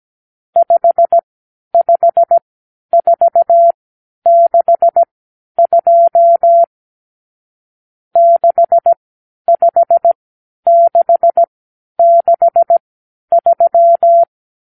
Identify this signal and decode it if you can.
Morse code, heard as a keyed tone, about 17 wpm, spaced as for 10 wpm, 680 Hz. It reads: 55462 65663
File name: k7qo / Lesson 50 (1-6)